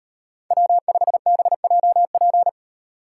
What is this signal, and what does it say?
Morse code, heard as a keyed tone, about 38 wpm, 700 Hz. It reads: W5BJP